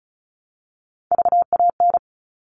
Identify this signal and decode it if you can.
Morse code, heard as a keyed tone, about 34 wpm, 700 Hz. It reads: VAN
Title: All Cut Numbers - 3-Digit 35wpm